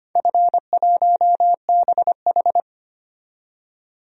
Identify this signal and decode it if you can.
Morse code, heard as a keyed tone, about 25 wpm, 700 Hz. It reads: F1BH
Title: DXCC-100 Call Signs - 4 Characters - 25wpm